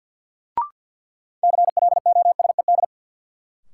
Morse code, heard as a keyed tone, about 53 wpm, 700 Hz. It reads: CLOSED
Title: Sentences from Top 1400 Words - New Word Drill - 50wpm